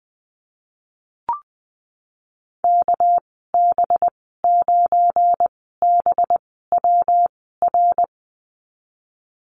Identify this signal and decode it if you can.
Morse code, heard as a keyed tone, about 20 wpm, spaced as for 15 wpm, 700 Hz. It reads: KB9BWR
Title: 200 Lesson 32g - Letter k - Call Signs 20wpm